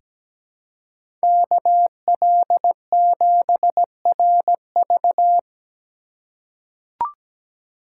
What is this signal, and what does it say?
Morse code, heard as a keyed tone, about 17 wpm, 700 Hz. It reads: KL7RV
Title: USA Call Signs - 5 Characters - 4x Character Spacing - 17wpm